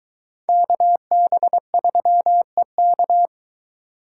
Morse code, heard as a keyed tone, about 21 wpm, 700 Hz. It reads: KB3EK